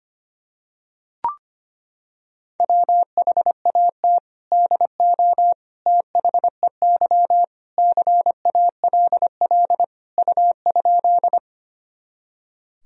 Morse code, heard as a keyed tone, about 25 wpm, 700 Hz. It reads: WHAT DO THEY CALL U?
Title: Sentences from Top 100 Words and Abbreviations 25wpm